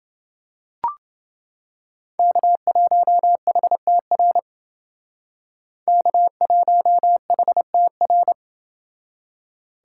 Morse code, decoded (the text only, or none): K1HTR K1HTR